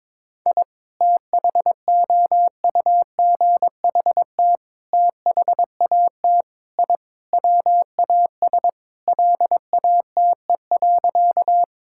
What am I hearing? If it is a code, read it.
Morse code, heard as a keyed tone, about 21 wpm, 700 Hz. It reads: I THOUGHT THAT I WAS LATE.
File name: practice / 22wpm / Sentences with 3-Word Phrases - 22wpm